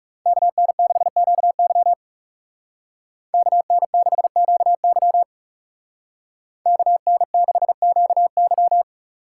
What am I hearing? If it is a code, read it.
Morse code, heard as a keyed tone, about 37 wpm, 700 Hz. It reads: KN6QY KN6QY KN6QY
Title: Speed Racing - USA 2x2 Call Signs 30wpm